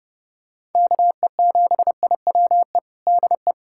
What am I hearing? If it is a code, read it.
Morse code, heard as a keyed tone, about 30 wpm, 700 Hz. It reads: KE7IWE DE
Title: USA Callsign DE Callsign 30wpm